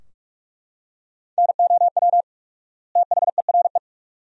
Morse code, heard as a keyed tone, about 41 wpm, 700 Hz. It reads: NOW THERE
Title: 4x Word-Spacing - Sentences from Top 100 Words - 45wpm